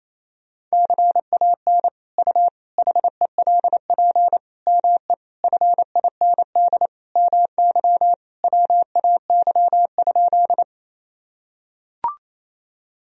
Morse code, heard as a keyed tone, about 28 wpm, 700 Hz. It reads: CAN U HELP ME FIND MY WAY?
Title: Sentences from Top 200 Words and Abbreviations 28wpm